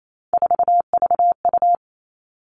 Morse code, heard as a keyed tone, about 28 wpm, 700 Hz. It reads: 4VU
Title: Three Challenging Letter-Number Combinations 28wpm